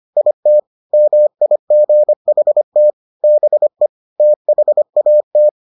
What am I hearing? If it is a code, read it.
Morse code, heard as a keyed tone, about 25 wpm, 600 Hz. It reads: IT MIGHT BE THAT